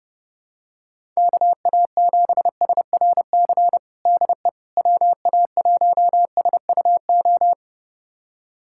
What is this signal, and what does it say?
Morse code, heard as a keyed tone, about 31 wpm, 700 Hz. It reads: KA7SRC DE WA1SUO
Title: USA Callsign DE Callsign - 6 Characters - 30wpm